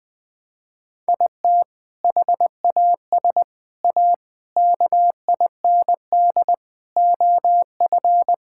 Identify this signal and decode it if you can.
Morse code, heard as a keyed tone, about 20 wpm, 700 Hz. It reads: IT HAS A KIND OF
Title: Sentences from Top 1000 Words - Review Part 5 - 20wpm